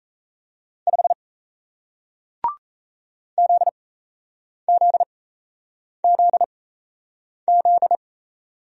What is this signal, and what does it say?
Morse code, heard as a keyed tone, about 38 wpm, spaced as for 8 wpm, 700 Hz. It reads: 5 ZZZZ